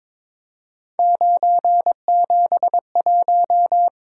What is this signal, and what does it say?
Morse code, heard as a keyed tone, about 22 wpm, 700 Hz. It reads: 971
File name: practice / 22wpm / Three Letter-Number Combinations - 22wpm